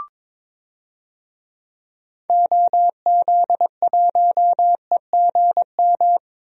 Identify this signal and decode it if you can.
Morse code, heard as a keyed tone, about 21 wpm, 700 Hz. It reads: OZ1EGM
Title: DXCC-100 Call Signs - 6 Characters - Repeated 3x - 22wpm